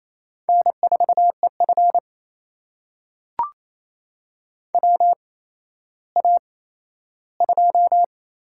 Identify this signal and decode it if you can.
Morse code, heard as a keyed tone, about 28 wpm, spaced as for 14 wpm, 700 Hz. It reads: N4EF WA2
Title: USA Call Signs - 4 Characters - 8x Character Spacing - 28wpm